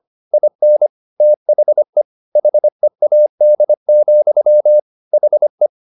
Morse code, heard as a keyed tone, about 25 wpm, 600 Hz. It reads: IN THE HEAD, HE